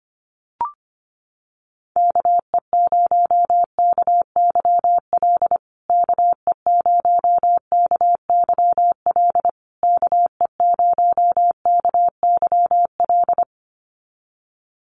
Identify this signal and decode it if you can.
Morse code, heard as a keyed tone, about 25 wpm, 700 Hz. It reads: KE0KYL KE0KYL KE0KYL
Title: USA Call Signs Repeated 3x 25wpm